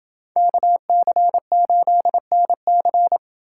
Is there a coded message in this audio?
Yes: Morse code from 0.4 to 3.2 s, heard as a keyed tone, about 28 wpm, 700 Hz. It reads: KC8NC